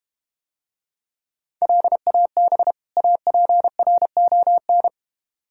Morse code, heard as a keyed tone, about 32 wpm, 700 Hz. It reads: LAB APRON